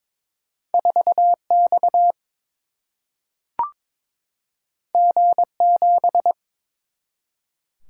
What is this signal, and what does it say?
Morse code, heard as a keyed tone, about 22 wpm, 700 Hz. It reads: 4X G7
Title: Single-Letter Single-Number 22wpm